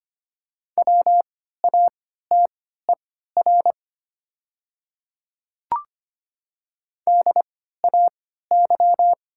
Morse code, heard as a keyed tone, about 25 wpm, spaced as for 14 wpm, 700 Hz. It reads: WATER DAY
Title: 3x Character Spacing - Top 100 Words - 25wpm